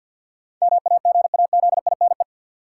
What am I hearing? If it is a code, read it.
Morse code, heard as a keyed tone, about 52 wpm, 700 Hz. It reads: MAGAZINE
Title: Sentences from Top 1000 Words - Review Part 7 - New Word Drill - 50wpm